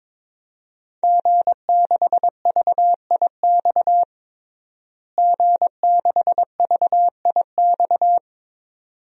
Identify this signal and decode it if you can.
Morse code, heard as a keyed tone, about 22 wpm, 700 Hz. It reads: G6VIX G6VIX
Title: DXCC-100 Call Signs - 5 Characters - Repeated 2x - 22wpm